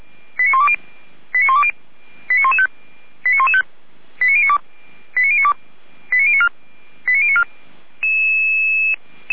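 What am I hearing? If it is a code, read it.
Five-tone selective call (ZVEI), heard as quick runs of five tones, tones of about 70 ms. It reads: 89120 89120 89115 89115 89092 89092 89094 89094
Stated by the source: ZVEI five-tone selective call received on an SDR.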